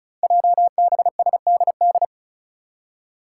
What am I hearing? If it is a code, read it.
Morse code, heard as a keyed tone, about 36 wpm, 700 Hz. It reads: JBSDD
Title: Five Challenging Letter-Number Combinations 35wpm